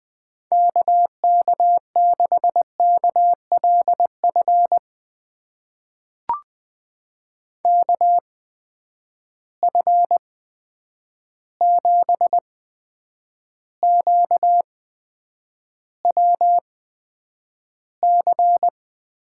Morse code, heard as a keyed tone, about 20 wpm, spaced as for 9 wpm, 700 Hz. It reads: KK6KLF KF7QWC
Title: USA Call Signs - 6 Characters - 8x Character Spacing - 20wpm